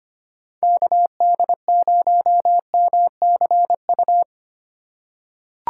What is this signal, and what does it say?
Morse code, heard as a keyed tone, about 25 wpm, 700 Hz. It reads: KD0MCU
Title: USA Call Signs - 6 Characters - 2x Character Spacing - 25wpm